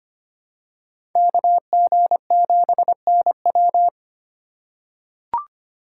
Morse code, heard as a keyed tone, about 25 wpm, 700 Hz. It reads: KG7NW
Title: USA Call Signs - 5 Characters - 2x Character Spacing - 25wpm